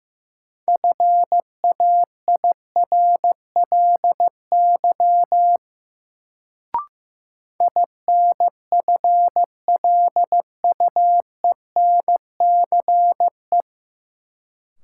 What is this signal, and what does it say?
Morse code, heard as a keyed tone, about 15 wpm, 700 Hz. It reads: FAIRLY INFLUENCE
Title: Sentences from Top 1100 Words - New Word Drill Alt - 15wpm